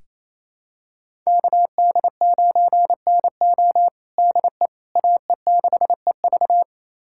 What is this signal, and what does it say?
Morse code, heard as a keyed tone, about 27 wpm, 700 Hz. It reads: KD9NO DE AE6EV
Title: USA Callsign DE Callsign - 5 Characters - 28wpm